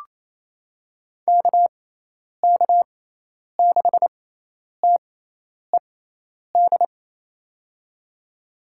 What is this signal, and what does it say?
Morse code, heard as a keyed tone, about 28 wpm, spaced as for 10 wpm, 700 Hz. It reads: KK6TED
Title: USA Call Signs - 6x Character Spacing - 28wpm